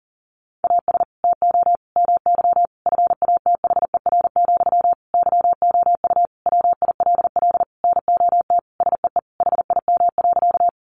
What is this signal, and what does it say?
Morse code, heard as a keyed tone, about 40 wpm, 700 Hz. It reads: AS TO MY FATHER, YOU WILL NOT SEE HIM.
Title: Sentences from Top 600 Words 40wpm